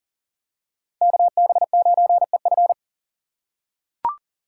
Morse code, heard as a keyed tone, about 40 wpm, 700 Hz. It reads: KB9EF
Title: USA Call Signs - 2x Character Spacing - 40wpm